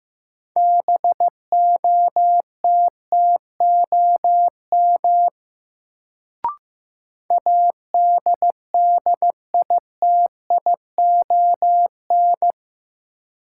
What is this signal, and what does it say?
Morse code, heard as a keyed tone, about 15 wpm, 700 Hz. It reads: BOTTOM ADDITION
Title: Sets of 3 Words - New Word Drill - 15wpm